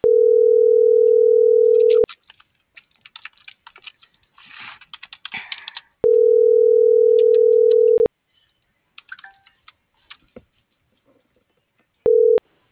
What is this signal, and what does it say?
Ringback tone (North America), 440 + 480 Hz.
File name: CENTRAL MABE / out karen.maya